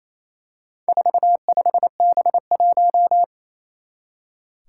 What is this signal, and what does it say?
Morse code, heard as a keyed tone, about 28 wpm, 700 Hz. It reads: 45B1
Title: Four Challenging Letter-Number Combinations 28wpm